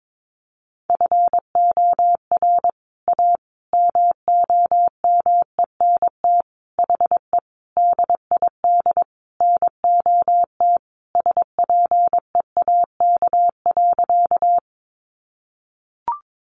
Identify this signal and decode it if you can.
Morse code, heard as a keyed tone, about 22 wpm, 700 Hz. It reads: FOR A MOMENT HE DID NOT SPEAK.
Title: Sentences from Top 1100 Words 22wpm